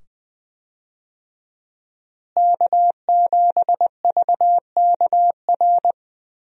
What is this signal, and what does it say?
Morse code, heard as a keyed tone, about 20 wpm, 700 Hz. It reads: K7VKR
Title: USA Call Signs - 5 Characters - Repeated 2x - 20wpm